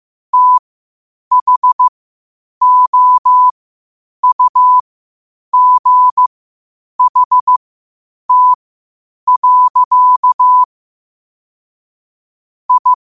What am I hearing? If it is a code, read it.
Morse code, heard as a keyed tone, about 15 wpm, spaced as for 8 wpm, 1000 Hz. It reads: THOUGHT. I
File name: qotdfiles / 8wpm